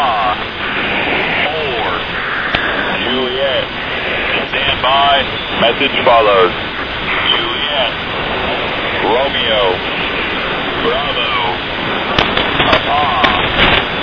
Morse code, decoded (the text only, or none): none